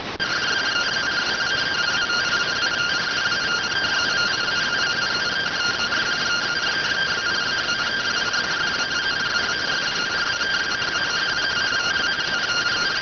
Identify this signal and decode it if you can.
Two-tone teleprinter signal, about 170 Hz shift.